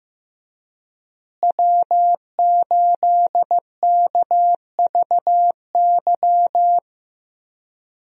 Morse code, heard as a keyed tone, about 15 wpm, 700 Hz. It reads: W8KVY
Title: USA Call Signs - 5 Characters - Repeated 2x - 15wpm